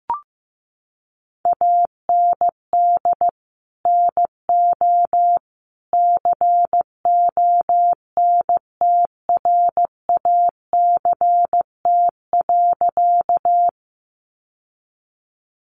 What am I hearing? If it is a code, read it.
Morse code, heard as a keyed tone, about 15 wpm, 700 Hz. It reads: AND NO CONTRACT.
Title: Sentences from Top 1400 Words 15wpm